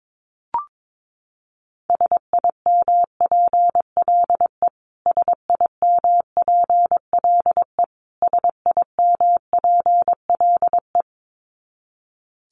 Morse code, heard as a keyed tone, about 22 wpm, 700 Hz. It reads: SIMPLE SIMPLE SIMPLE
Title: Top 500 Words Repeated 22wpm